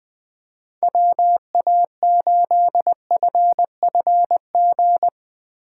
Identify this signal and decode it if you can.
Morse code, heard as a keyed tone, about 21 wpm, 700 Hz. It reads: WA8FFG
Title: Speed Racing - USA Call Signs 20wpm